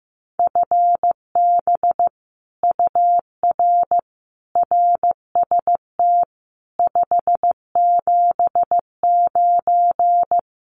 Morse code, heard as a keyed tone, about 15 wpm, 700 Hz. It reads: FB UR RST 579
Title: Common QSO Phrases Encore 15wpm